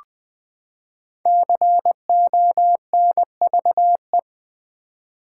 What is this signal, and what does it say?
Morse code, heard as a keyed tone, about 20 wpm, 700 Hz. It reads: CONVE
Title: Build A Word - 5 Letters At A Time - 12-Letter Words - 20wpm